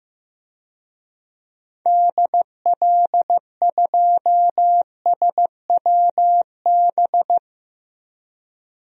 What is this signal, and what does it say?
Morse code, heard as a keyed tone, about 15 wpm, 700 Hz. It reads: DL2SWB
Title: DXCC-100 Call Signs - 6 Characters - 15wpm